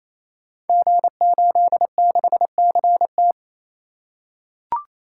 Morse code, heard as a keyed tone, about 28 wpm, 700 Hz. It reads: G86CT
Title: Code Groups - 5 Letter:Numbers - 28wpm